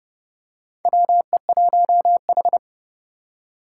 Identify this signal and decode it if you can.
Morse code, heard as a keyed tone, about 28 wpm, 700 Hz. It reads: WE1H
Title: Four Challenging Letter-Number Combinations 30wpm